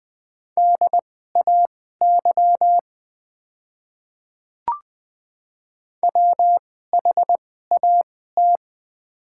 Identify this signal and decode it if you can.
Morse code, heard as a keyed tone, about 20 wpm, spaced as for 15 wpm, 690 Hz. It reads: DAY WHAT
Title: 2x Character Spacing - Top 100 Words - 20wpm